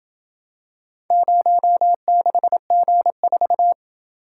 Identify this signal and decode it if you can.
Morse code, heard as a keyed tone, about 28 wpm, 700 Hz. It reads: 06G4